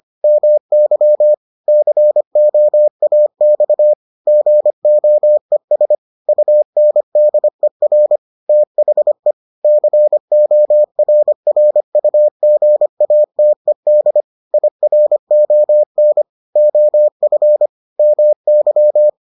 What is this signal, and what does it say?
Morse code, heard as a keyed tone, about 25 wpm, 600 Hz. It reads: MY COAX GOES UNDER THE CORRUGATED IRON OF MY